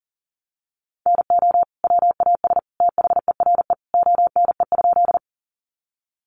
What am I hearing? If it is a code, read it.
Morse code, heard as a keyed tone, about 40 wpm, 700 Hz. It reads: NO WAS THERE ONE?
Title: Sentences from Top 100 Words 40wpm